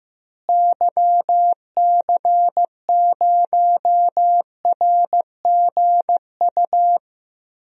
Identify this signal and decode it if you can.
Morse code, heard as a keyed tone, about 15 wpm, 700 Hz. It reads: YC0RGU
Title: DXCC-100 Call Signs - 6 Characters - Repeated 2x - 15wpm